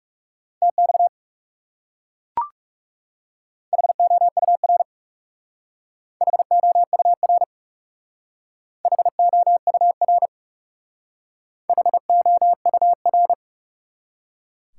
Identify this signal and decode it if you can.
Morse code, heard as a keyed tone, about 37 wpm, 700 Hz. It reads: TX HOUR HOUR HOUR HOUR
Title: Speed-Racing Top 180 QSO Elements 30wpm